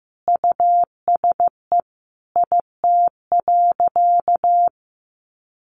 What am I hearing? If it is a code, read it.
Morse code, heard as a keyed tone, about 15 wpm, 700 Hz. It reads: USE IT.
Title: Sentences from Top 1100 Words 15wpm